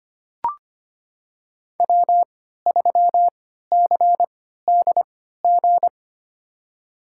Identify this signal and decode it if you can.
Morse code, heard as a keyed tone, about 25 wpm, spaced as for 14 wpm, 700 Hz. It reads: W3CDG
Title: USA Call Signs - 3x Character Spacing - 25wpm